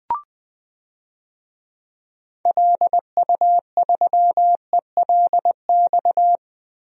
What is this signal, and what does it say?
Morse code, heard as a keyed tone, about 21 wpm, 700 Hz. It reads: LU3ELX
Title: DXCC-100 Call Signs - 6 Characters - Repeated 3x - 20wpm